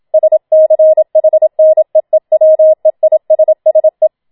Morse code, heard as a keyed tone, about 27 wpm, 630 Hz. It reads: SCHNEEWEISSE
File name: Schneewittchen Teil2 27wpm